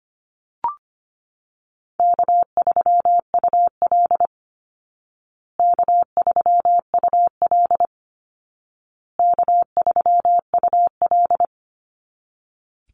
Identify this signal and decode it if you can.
Morse code, heard as a keyed tone, about 25 wpm, 700 Hz. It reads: K3UL K3UL K3UL
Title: Easy Call Signs Repeated 3x - 25wpm